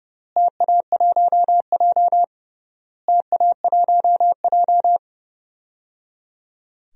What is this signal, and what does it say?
Morse code, heard as a keyed tone, about 30 wpm, 700 Hz. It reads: TA1J TA1J
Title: DXCC-100 Call Signs - 4 Characters - Repeated 3x - 30wpm